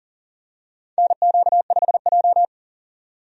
Morse code, heard as a keyed tone, about 40 wpm, 700 Hz. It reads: NQ5J
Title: USA Call Signs - 4 Characters - 40wpm